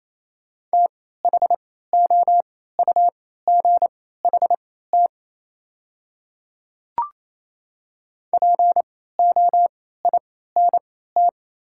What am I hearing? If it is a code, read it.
Morse code, heard as a keyed tone, about 28 wpm, spaced as for 16 wpm, 700 Hz. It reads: THOUGHT POINT
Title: 3x Character Spacing - Top 300 Words - 28wpm